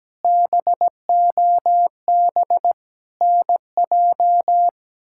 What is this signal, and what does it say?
Morse code, heard as a keyed tone, about 17 wpm, 700 Hz. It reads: BOB NJ
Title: K1USN SST QSO Practice - 17wpm